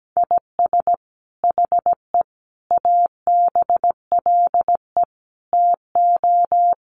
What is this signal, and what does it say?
Morse code, heard as a keyed tone, about 17 wpm, 700 Hz. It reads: IS HE ABLE TO
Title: Sentences from Top 1400 Words 17wpm